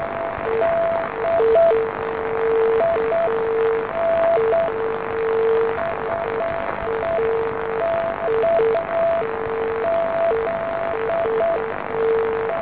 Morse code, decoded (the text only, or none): none